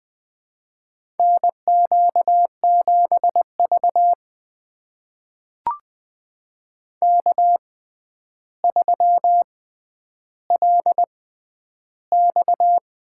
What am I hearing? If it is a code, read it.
Morse code, heard as a keyed tone, about 20 wpm, spaced as for 10 wpm, 700 Hz. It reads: NQ7V K3LX
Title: USA Call Signs - 4 Characters - 6x Character Spacing - 20wpm